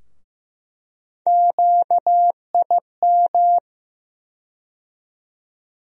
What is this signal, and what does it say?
Morse code, heard as a keyed tone, about 15 wpm, 700 Hz. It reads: QIM
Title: USA Call Sign Postfixes - 3 Letters 15wpm